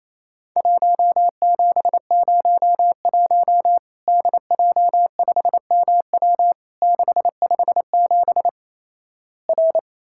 Morse code, heard as a keyed tone, about 28 wpm, 700 Hz. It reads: 1701 DJ5MW 657 R
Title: Worked All Europe DX Contest - QTC Exchange - 28wpm